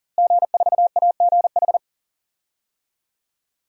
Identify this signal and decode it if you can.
Morse code, heard as a keyed tone, about 40 wpm, 700 Hz. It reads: G4AGH
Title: DXCC-100 Call Signs - 5 Characters - Repeated 3x - 40wpm